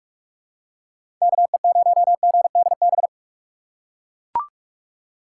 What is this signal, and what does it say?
Morse code, heard as a keyed tone, about 42 wpm, 700 Hz. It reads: KE0GDB
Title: USA Call Signs - 4x Character Spacing - 45wpm